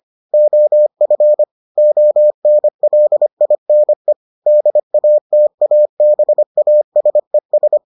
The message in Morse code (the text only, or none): OF ONLINE DATABASES